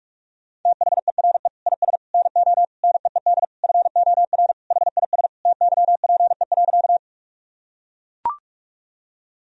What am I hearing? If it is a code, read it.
Morse code, heard as a keyed tone, about 42 wpm, 700 Hz. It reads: THERE IS NO NEED FOR HIS TYPE.
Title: Sentences from Top 1000 Words - Review Part 3 - 45wpm